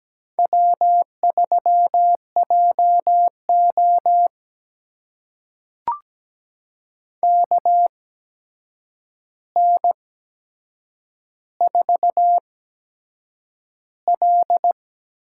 Morse code, heard as a keyed tone, about 17 wpm, spaced as for 7 wpm, 700 Hz. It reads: W3JO KN4L
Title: USA Call Signs - 4 Characters - 8x Character Spacing - 17wpm